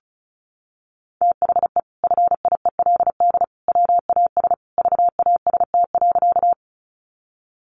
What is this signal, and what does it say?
Morse code, heard as a keyed tone, about 34 wpm, 700 Hz. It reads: THE FIELD WAS VAST.
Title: Sentences from Top 1300 Words 35wpm